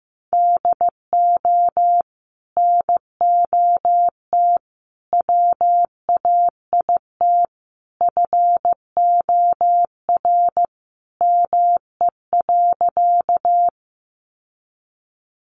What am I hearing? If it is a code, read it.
Morse code, heard as a keyed tone, about 15 wpm, 700 Hz. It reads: DO NOT WAIT FOR ME.
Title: Sentences from Top 600 Words 15wpm